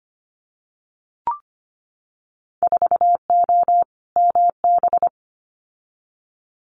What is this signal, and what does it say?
Morse code, heard as a keyed tone, about 25 wpm, 700 Hz. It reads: 4O MB
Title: Winter Field Day Exchanges Encore 25wpm